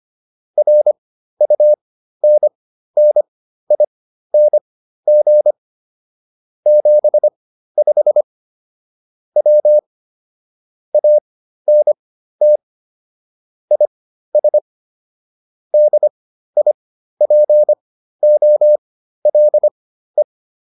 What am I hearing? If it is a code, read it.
Morse code, heard as a keyed tone, about 25 wpm, spaced as for 13 wpm, 600 Hz. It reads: RUNNING 75 W ANT IS DIPOLE